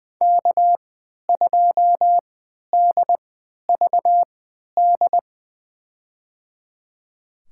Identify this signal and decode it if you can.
Morse code, heard as a keyed tone, about 20 wpm, spaced as for 11 wpm, 700 Hz. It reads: K2DVD
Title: USA Call Signs - 5 Characters - 3x Character Spacing - 20wpm